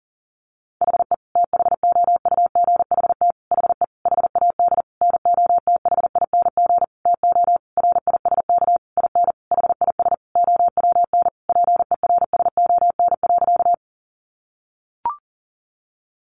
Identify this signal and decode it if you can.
Morse code, heard as a keyed tone, about 40 wpm, 700 Hz. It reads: HE THOUGHT HE HAD NOTHING TO RISK IN HIS OWN PERSON.
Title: Sentences from Top 600 Words 40wpm